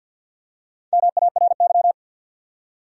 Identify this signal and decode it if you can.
Morse code, heard as a keyed tone, about 48 wpm, 700 Hz. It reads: MARY
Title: USA Common Names - Max 5 Letters - 50wpm